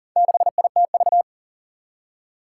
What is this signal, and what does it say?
Morse code, heard as a keyed tone, about 40 wpm, 700 Hz. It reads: 6ITV
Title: Four Challenging Letter-Number Combinations 40wpm